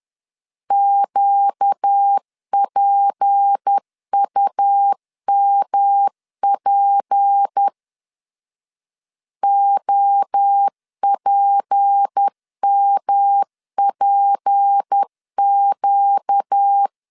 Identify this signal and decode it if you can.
Morse code, heard as a keyed tone, about 10 wpm, 800 Hz. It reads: QPUMP OPMPQ
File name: LEZIONI / 6 PKX0 / 6 P 10wpm 2432